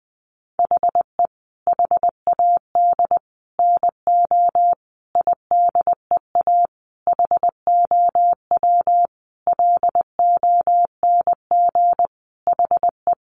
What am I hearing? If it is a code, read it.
Morse code, heard as a keyed tone, about 20 wpm, 700 Hz. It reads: HE HAD NO IDEA HOW LONG HE
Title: Sentences from Top 300 Words 20wpm